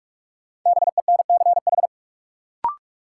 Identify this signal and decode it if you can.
Morse code, heard as a keyed tone, about 42 wpm, 700 Hz. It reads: BENCH
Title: Sentences from Top 1400 Words - New Word Drill - 45wpm